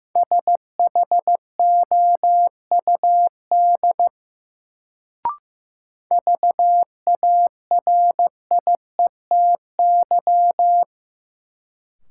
Morse code, heard as a keyed tone, about 15 wpm, 700 Hz. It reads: SHOUD VARIETY
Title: Sentences from Top 1000 Words - Review Part 4 - New Word Drill - 15wpm